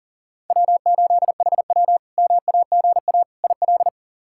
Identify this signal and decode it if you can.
Morse code, heard as a keyed tone, about 40 wpm, 700 Hz. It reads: W8HW MAGA IL
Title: K1USN SST QSO Practice - 40wpm